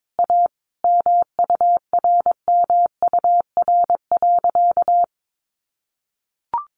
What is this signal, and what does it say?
Morse code, heard as a keyed tone, about 22 wpm, 700 Hz. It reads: A MURMUR.
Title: A Sound Of Thunder 22wpm